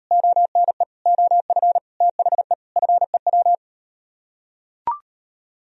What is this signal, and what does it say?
Morse code, heard as a keyed tone, about 38 wpm, 700 Hz. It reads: ONE OF THE FEW